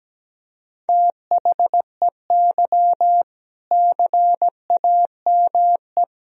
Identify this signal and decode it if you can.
Morse code, heard as a keyed tone, about 17 wpm, 700 Hz. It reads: THEY CAME
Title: Sentences from Top 1000 Words - Review Part 3 - 17wpm